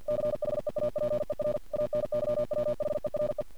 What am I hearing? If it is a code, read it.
Morse code, heard as a keyed tone, about 51 wpm, 600 Hz. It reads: CHEAPER ANYWHERE
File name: archived / CRSnet